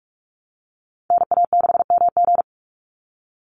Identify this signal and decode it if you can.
Morse code, heard as a keyed tone, about 44 wpm, 700 Hz. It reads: NA6MG
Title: NA SOTA Chaser Callsigns 45wpm